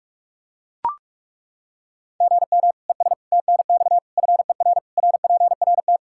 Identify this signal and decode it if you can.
Morse code, heard as a keyed tone, about 43 wpm, 700 Hz. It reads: GM ES TNX FER RPRT
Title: K1USN SST QSO Practice 45wpm